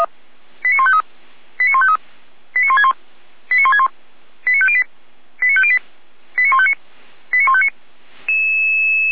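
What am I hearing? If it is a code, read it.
Five-tone selective call (ZVEI), heard as quick runs of five tones, tones of about 70 ms. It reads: 89153 89153 89161 89161 89508 89508 89159 89159
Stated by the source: ZVEI five-tone call received by SDR, from the Feuerwehr Orlamünde audio server.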